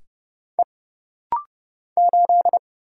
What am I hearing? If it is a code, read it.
Morse code, heard as a keyed tone, about 30 wpm, 700 Hz. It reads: E 8
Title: Single Letter-Number Alt - Rapid-Fire 30wpm